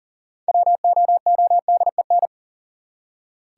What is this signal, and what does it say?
Morse code, heard as a keyed tone, about 37 wpm, 700 Hz. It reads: WOODEN
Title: Six Letter Words - 40wpm